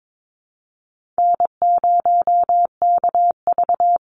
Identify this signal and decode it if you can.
Morse code, heard as a keyed tone, about 22 wpm, 700 Hz. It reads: N0KV
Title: Call Signs Repeated 3x - 22wpm